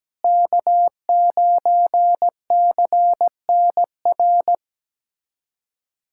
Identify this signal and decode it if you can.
Morse code, heard as a keyed tone, about 17 wpm, 690 Hz. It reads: K9CNR
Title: SOTA QSO Practice 17wpm